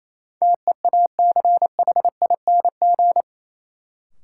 Morse code, heard as a keyed tone, about 28 wpm, 700 Hz. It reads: TEACHING
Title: Sentences from Top 1100 Words - New Word Drill Alt - 28wpm